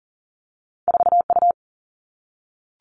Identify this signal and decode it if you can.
Morse code, heard as a keyed tone, about 38 wpm, 700 Hz. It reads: 4 U
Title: Two Challenging Letter-Number Combinations 40wpm